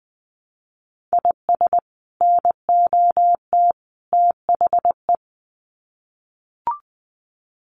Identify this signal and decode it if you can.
Morse code, heard as a keyed tone, about 20 wpm, 700 Hz. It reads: IS NOT THE
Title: Sets of 3 Words 20wpm